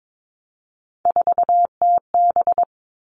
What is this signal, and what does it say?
Morse code, heard as a keyed tone, about 22 wpm, 700 Hz. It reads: 4TB
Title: All Cut Numbers - 3-Digit 22wpm